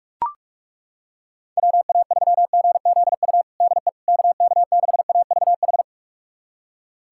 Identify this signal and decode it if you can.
Morse code, heard as a keyed tone, about 41 wpm, 700 Hz. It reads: WA3GZU DE KK6AVH
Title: USA Callsign DE Callsign - 6 Characters - 45wpm